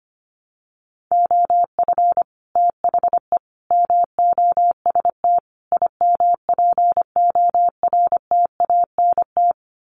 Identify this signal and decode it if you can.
Morse code, heard as a keyed tone, about 25 wpm, 700 Hz. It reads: OF THE MOST IMPORTANT
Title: Sets of 4 Words 25wpm